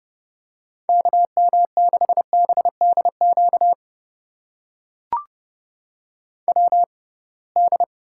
Morse code, heard as a keyed tone, about 30 wpm, spaced as for 23 wpm, 700 Hz. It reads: KM6BDQ WD
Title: USA Call Signs - 6 Characters - 6x Character Spacing - 30wpm